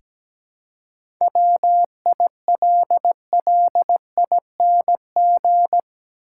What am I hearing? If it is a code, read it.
Morse code, heard as a keyed tone, about 17 wpm, 700 Hz. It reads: WILLING
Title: Seven Letter Words - 17wpm